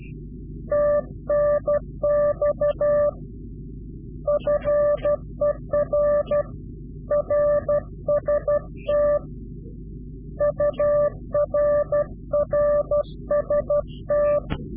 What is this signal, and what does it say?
Morse code, heard as a keyed tone, about 12 wpm, 590 Hz. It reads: TNX FER RST UR RST
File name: CW-QSO / Directional Call / Directional QSO